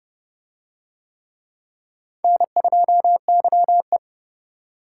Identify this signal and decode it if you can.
Morse code, heard as a keyed tone, about 28 wpm, 700 Hz. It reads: N2YE